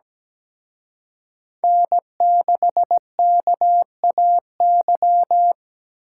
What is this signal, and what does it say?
Morse code, heard as a keyed tone, about 20 wpm, 700 Hz. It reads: N6KAY
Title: Speed Racing - USA Call Signs 17wpm